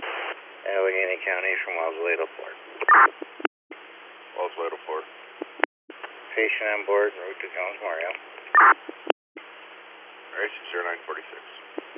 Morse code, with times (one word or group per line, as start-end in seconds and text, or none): none